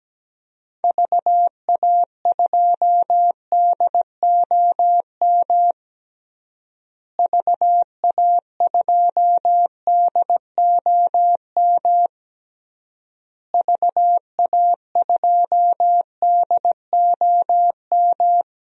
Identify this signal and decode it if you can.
Morse code, heard as a keyed tone, about 17 wpm, 700 Hz. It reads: VA2DOM VA2DOM VA2DOM
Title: DXCC-100 Call Signs - 6 Characters - Repeated 3x - 17wpm